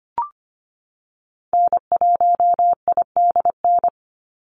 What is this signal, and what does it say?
Morse code, heard as a keyed tone, about 25 wpm, 690 Hz. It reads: N1IDN
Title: NA SOTA Activator Callsigns 25wpm